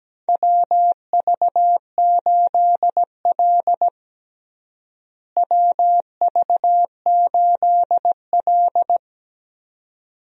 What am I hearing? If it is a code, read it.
Morse code, heard as a keyed tone, about 17 wpm, 700 Hz. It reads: WV8L WV8L
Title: USA Call Signs - 4 Characters - Repeated 2x - 17wpm